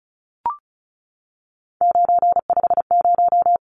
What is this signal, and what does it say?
Morse code, heard as a keyed tone, about 34 wpm, 700 Hz. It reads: 950
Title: Numbers - 3-Digit 35wpm